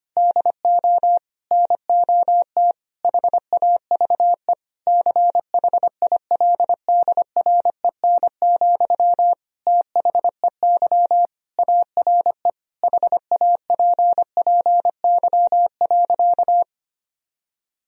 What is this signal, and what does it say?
Morse code, heard as a keyed tone, about 25 wpm, 700 Hz. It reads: DO NOT HAVE CHILDREN, THEY ARE HAPPY.
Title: Sentences from Top 1000 Words - Review Part 3 - 25wpm